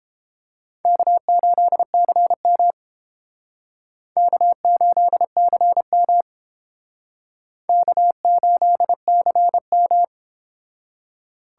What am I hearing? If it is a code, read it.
Morse code, heard as a keyed tone, about 30 wpm, 700 Hz. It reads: K8CM K8CM K8CM